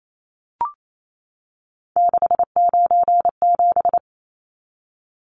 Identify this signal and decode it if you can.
Morse code, heard as a keyed tone, about 28 wpm, 700 Hz. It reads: 697
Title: Numbers - 3-Digit 28wpm